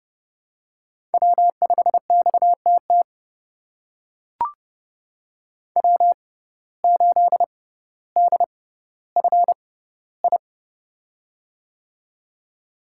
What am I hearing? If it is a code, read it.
Morse code, heard as a keyed tone, about 30 wpm, spaced as for 15 wpm, 700 Hz. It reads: W5XTT W8DFI
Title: USA Call Signs - 5 Characters - 6x Character Spacing - 30wpm